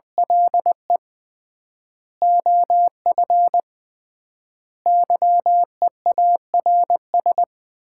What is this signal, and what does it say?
Morse code, heard as a keyed tone, about 20 wpm, 700 Hz. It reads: LE OF YEARS
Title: 3x Word-Spacing - Sets of 5 Words - 20wpm